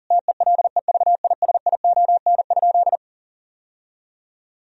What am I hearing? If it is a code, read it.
Morse code, heard as a keyed tone, about 40 wpm, 700 Hz. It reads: TELEVISION?
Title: Sentences from Top 1000 Words - Review Part 4 - 40wpm